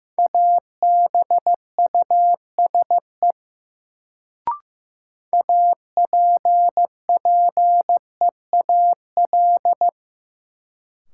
Morse code, heard as a keyed tone, about 15 wpm, 700 Hz. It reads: ABUSE APPEAL
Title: Sentences from Top 1300 Words - New Word Drill - 15wpm